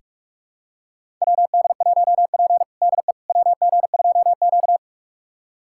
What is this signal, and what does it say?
Morse code, heard as a keyed tone, about 42 wpm, 700 Hz. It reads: WD1P DE WG2Q
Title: USA Callsign DE Callsign - 4 Characters - 45wpm